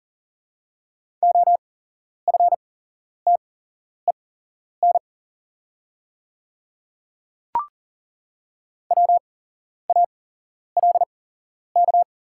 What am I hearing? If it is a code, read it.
Morse code, heard as a keyed tone, about 38 wpm, spaced as for 11 wpm, 700 Hz. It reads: OFTEN WALK
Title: 8x Character Spacing - Top 300 Words - 40wpm